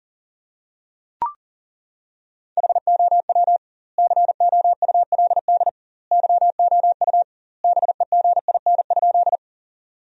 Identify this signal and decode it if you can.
Morse code, heard as a keyed tone, about 37 wpm, 700 Hz. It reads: HOW COULD YOU BEGIN?
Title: 2x Word-Spacing - Sentences from Top 300 Words - 40wpm